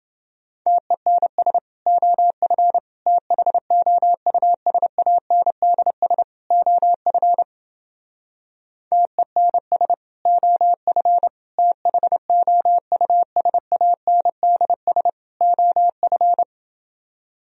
Morse code, heard as a keyed tone, about 29 wpm, 700 Hz. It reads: TENS OF THOUSANDS OF TENS OF THOUSANDS OF